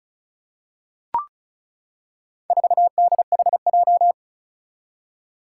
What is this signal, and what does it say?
Morse code, heard as a keyed tone, about 36 wpm, 700 Hz. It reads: 4DHJ
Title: Four Challenging Letter-Number Combinations 35wpm